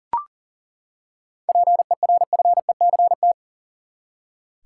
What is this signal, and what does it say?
Morse code, heard as a keyed tone, about 40 wpm, 700 Hz. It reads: PERFECT
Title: Seven Letter Words - 40wpm